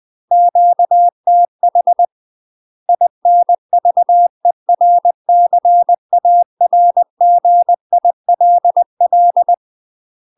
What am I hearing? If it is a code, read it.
Morse code, heard as a keyed tone, about 20 wpm, 700 Hz. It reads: QTH INVERCARGILL
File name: UK MASTER QSOs TRAINING / 20WPM / UK MASTER QSOs 0028